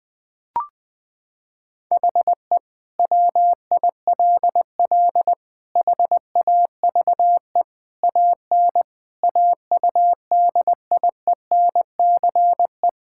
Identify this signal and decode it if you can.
Morse code, heard as a keyed tone, about 21 wpm, 700 Hz. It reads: HE WILL HAVE AN AUDIENCE
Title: Sentences from Top 1000 Words - Review Part 5 - 20wpm